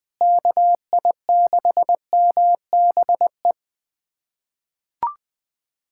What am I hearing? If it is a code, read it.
Morse code, heard as a keyed tone, about 21 wpm, 700 Hz. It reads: KI6MBE
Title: USA Callsign DE Callsign - 6 Characters - 20wpm